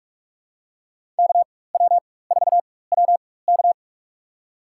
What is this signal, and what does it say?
Morse code, heard as a keyed tone, about 46 wpm, spaced as for 21 wpm, 700 Hz. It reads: KW4WK
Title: USA Call Signs - 4x Character Spacing - 45wpm